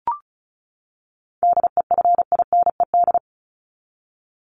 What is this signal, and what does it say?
Morse code, heard as a keyed tone, about 34 wpm, 700 Hz. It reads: DEFINED
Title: ED Suffix - Top 100 Words with Base Words 35wpm